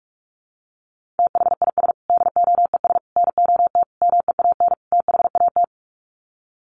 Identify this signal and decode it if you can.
Morse code, heard as a keyed tone, about 44 wpm, 700 Hz. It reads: THIS DOES NOT MEAN THAT
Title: Sets of 5 Words Encore2 45wpm